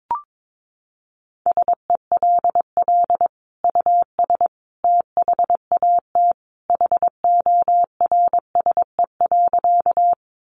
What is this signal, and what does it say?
Morse code, heard as a keyed tone, about 22 wpm, 700 Hz. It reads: SELL US THAT HORSE.
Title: Sentences from Top 1100 Words 22wpm